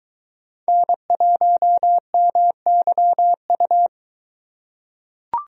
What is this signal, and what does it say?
Morse code, heard as a keyed tone, about 22 wpm, 700 Hz. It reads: N1MYU